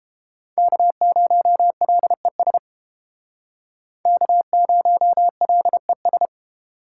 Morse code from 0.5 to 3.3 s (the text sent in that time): K0LES